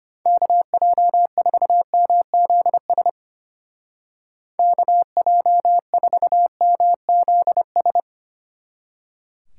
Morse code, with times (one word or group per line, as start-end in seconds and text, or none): 0.3-3.1 KJ4MZS
4.6-8.0 KJ4MZS